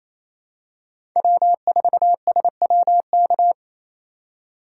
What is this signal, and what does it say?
Morse code, heard as a keyed tone, about 28 wpm, 700 Hz. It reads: W4SWK
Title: USA Call Signs - 5 Characters - 4x Character Spacing - 28wpm